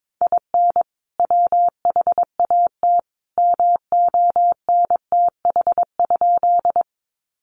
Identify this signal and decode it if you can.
Morse code, heard as a keyed tone, about 22 wpm, 700 Hz. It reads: IN WHAT MONTH?
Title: Sentences from Top 800 Words 22wpm